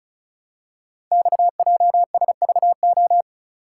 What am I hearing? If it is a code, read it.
Morse code, heard as a keyed tone, about 35 wpm, 700 Hz. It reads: XJSVO
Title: Code Groups - 5 Letters - 35wpm